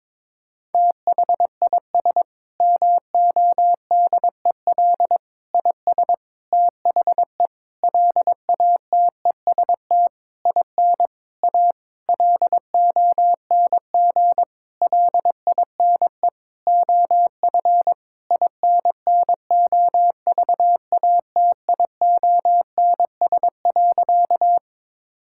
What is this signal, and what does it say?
Morse code, heard as a keyed tone, about 23 wpm, 700 Hz. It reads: THIS MODEL IS THE LATEST IN A LONG LINE OF INNOVATIONS.
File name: Sentences with 7-Word Phrases - 22wpm